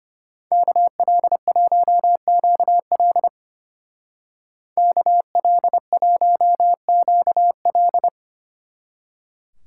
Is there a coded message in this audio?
Yes: Morse code from 0.5 to 8.1 s, heard as a keyed tone, about 32 wpm, 700 Hz. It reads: KL1QL KL1QL